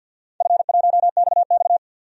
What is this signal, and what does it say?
Morse code, heard as a keyed tone, about 48 wpm, 700 Hz. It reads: F1XX
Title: DXCC-100 Call Signs - 4 Characters - 50wpm